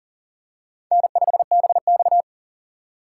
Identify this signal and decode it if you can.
Morse code, heard as a keyed tone, about 40 wpm, 700 Hz. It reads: N5BX
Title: USA Call Signs - 4 Characters - 2x Character Spacing - 40wpm